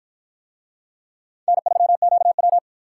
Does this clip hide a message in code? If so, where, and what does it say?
Yes: Morse code from 1.5 to 2.6 s, heard as a keyed tone, about 50 wpm, 700 Hz. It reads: N3QW